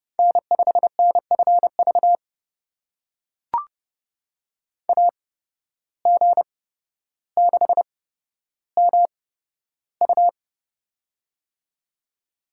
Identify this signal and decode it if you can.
Morse code, heard as a keyed tone, about 32 wpm, spaced as for 13 wpm, 700 Hz. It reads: N5NFV AG6MU
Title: USA Call Signs - 5 Characters - 8x Character Spacing - 30wpm